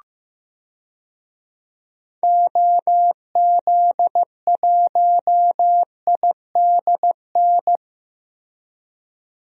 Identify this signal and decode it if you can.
Morse code, heard as a keyed tone, about 15 wpm, 700 Hz. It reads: OZ1IDN
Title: DXCC-100 Call Signs - 6 Characters - Repeated 3x - 15wpm